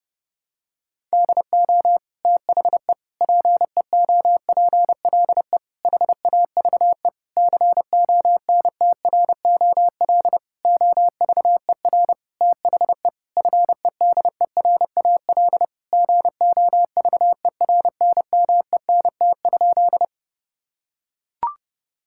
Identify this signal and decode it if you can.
Morse code, heard as a keyed tone, about 30 wpm, 700 Hz. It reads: DO THE PEOPLE HAVE CONTROL OVER THE FEDERAL GOVERNMENT?
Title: Sentences from Top 1000 Words - Review Part 5 - 30wpm